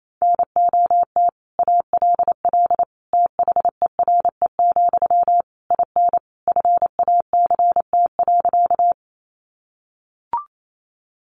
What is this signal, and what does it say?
Morse code, heard as a keyed tone, about 28 wpm, 700 Hz. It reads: NOT ALL THERE, IN FACT.
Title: Sentences from Top 600 Words 28wpm